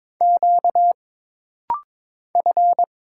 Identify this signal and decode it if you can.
Morse code, heard as a keyed tone, about 22 wpm, 700 Hz. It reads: Q F
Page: This rapid-fire timing is geared to push your ICR (Instant Character Recognition) skills.